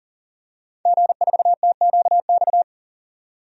Morse code, heard as a keyed tone, about 40 wpm, 700 Hz. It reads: G4TQX
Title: DXCC-100 Call Signs - 5 Characters - 40wpm